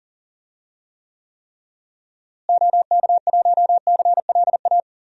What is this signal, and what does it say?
Morse code, heard as a keyed tone, about 40 wpm, 700 Hz. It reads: OK1CLA
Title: DXCC-100 Call Signs - 6 Characters - Repeated 3x - 40wpm